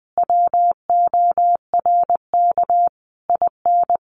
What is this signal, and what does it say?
Morse code, heard as a keyed tone, about 20 wpm, 700 Hz. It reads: WORK IN
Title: Sentences from Top 1500 Words 20wpm